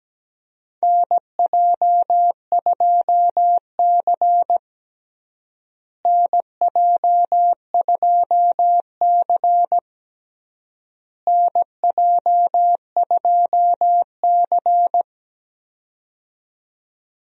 Morse code, heard as a keyed tone, about 17 wpm, 700 Hz. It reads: NJ2C NJ2C NJ2C
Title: USA Call Signs - 4 Characters - Repeated 3x - 17wpm